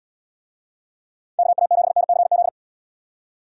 Morse code, heard as a keyed tone, about 78 wpm, 700 Hz. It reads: KA7NCZ